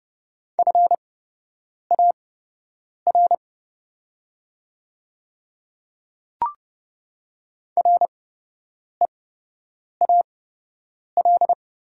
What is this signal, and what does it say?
Morse code, heard as a keyed tone, about 30 wpm, spaced as for 8 wpm, 700 Hz. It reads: FAR REAL
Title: 8x Character Spacing - Top 300 Words - 30wpm